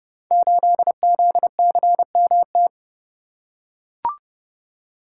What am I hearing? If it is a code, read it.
Morse code, heard as a keyed tone, about 28 wpm, 700 Hz. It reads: 8ZCMT
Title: Code Groups - 5 Letter:Numbers - 30wpm